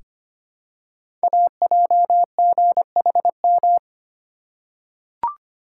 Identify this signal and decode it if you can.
Morse code, heard as a keyed tone, about 25 wpm, 700 Hz. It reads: AJGHM
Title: Code Groups - 5 Letters - 25wpm